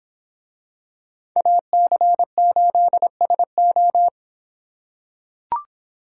Morse code, heard as a keyed tone, about 26 wpm, 700 Hz. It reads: AC8SO